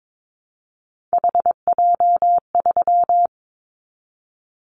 Morse code, heard as a keyed tone, about 22 wpm, 700 Hz. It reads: HJ3
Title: Callsign Prefixes for Top 100 DXCC - 22wpm